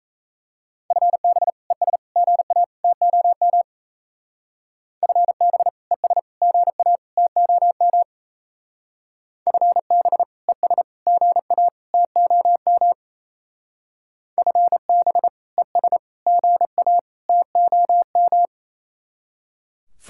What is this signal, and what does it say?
Morse code, heard as a keyed tone, about 35 wpm, 700 Hz. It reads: FB ES GA TOM FB ES GA TOM FB ES GA TOM FB ES GA TOM
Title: Speed-Racing Common QSO Phrases - 28wpm